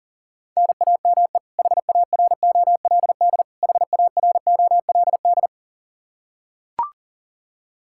Morse code, heard as a keyed tone, about 40 wpm, 700 Hz. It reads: NAME HAROLD HAROLD
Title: K1USN SST QSO Practice 40wpm